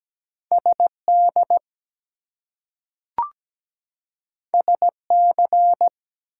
Morse code, heard as a keyed tone, about 17 wpm, 700 Hz. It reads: SD SC
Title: US States and Canadian Provinces 17wpm